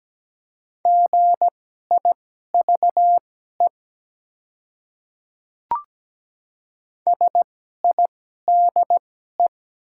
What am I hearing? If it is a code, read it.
Morse code, heard as a keyed tone, about 17 wpm, spaced as for 12 wpm, 700 Hz. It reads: GIVE SIDE
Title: 2x Character Spacing - Top 200 Words - 17wpm